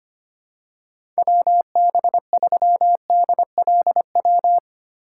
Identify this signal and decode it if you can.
Morse code, heard as a keyed tone, about 25 wpm, 700 Hz. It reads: WB3DLW
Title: USA Call Signs - 6 Characters - 2x Character Spacing - 25wpm